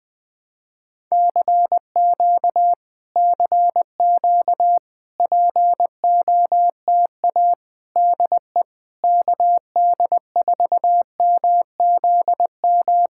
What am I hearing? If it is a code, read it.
Morse code, heard as a keyed tone, about 21 wpm, 700 Hz. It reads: CQ CQ POTA DE KD4MZM
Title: POTA QSO Practice - 20wpm